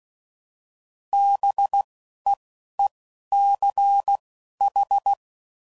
Morse code, heard as a keyed tone, about 16 wpm, spaced as for 11 wpm, 800 Hz. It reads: BEECH
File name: CW Code Groups @10WPM / fromSM's / Cities